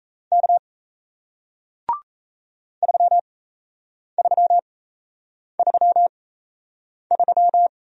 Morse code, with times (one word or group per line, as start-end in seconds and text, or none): 0.3-0.6 K
2.8-7.7 3333